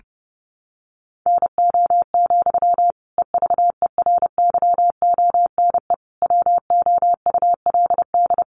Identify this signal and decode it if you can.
Morse code, heard as a keyed tone, about 30 wpm, 700 Hz. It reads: NO, EVERYONE WOULD
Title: Sentences from Top 800 Words 30wpm